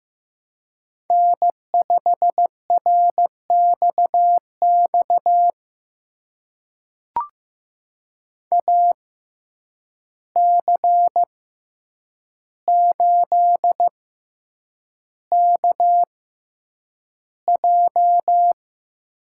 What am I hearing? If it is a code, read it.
Morse code, heard as a keyed tone, about 15 wpm, spaced as for 8 wpm, 700 Hz. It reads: N5RXX AC8KJ
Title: USA Call Signs - 5 Characters - 6x Character Spacing - 15wpm